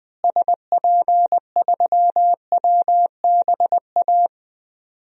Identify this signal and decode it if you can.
Morse code, heard as a keyed tone, about 20 wpm, 700 Hz. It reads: SP3WBA
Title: DXCC-100 Call Signs - 6 Characters - Repeated 2x - 20wpm